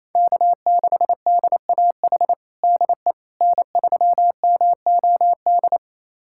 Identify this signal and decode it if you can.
Morse code, heard as a keyed tone, about 28 wpm, 700 Hz. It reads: K6DAH DE N3MOD
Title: USA Callsign DE Callsign - 5 Characters - 28wpm